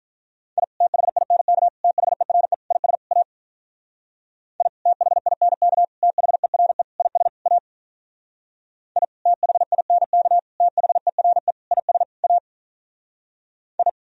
Morse code, heard as a keyed tone, about 43 wpm, 700 Hz. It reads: I THINK THERE IS A I THINK THERE IS A I THINK THERE IS A I
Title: Speed-Racing - Sets of 5 Words 35wpm